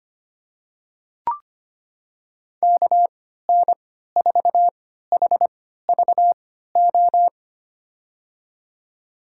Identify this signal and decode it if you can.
Morse code, heard as a keyed tone, about 25 wpm, spaced as for 14 wpm, 700 Hz. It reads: KN4HVO
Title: USA Call Signs - 6 Characters - 3x Character Spacing - 25wpm